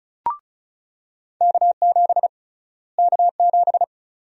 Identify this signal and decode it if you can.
Morse code, heard as a keyed tone, about 36 wpm, 700 Hz. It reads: K7 K7
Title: USA Call Sign Prefixes - Repeated 2x - 35wpm